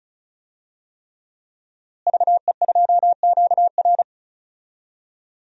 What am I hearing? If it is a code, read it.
Morse code, heard as a keyed tone, about 36 wpm, 700 Hz. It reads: VE2QR
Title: DXCC-100 Call Signs - 5 Characters - Repeated 3x - 35wpm